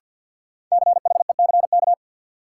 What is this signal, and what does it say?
Morse code, heard as a keyed tone, about 50 wpm, 700 Hz. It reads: CHECK
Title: Sentences from Top 1400 Words - New Word Drill Alt - 50wpm